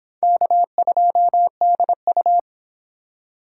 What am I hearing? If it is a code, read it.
Morse code, heard as a keyed tone, about 31 wpm, 700 Hz. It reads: K2DU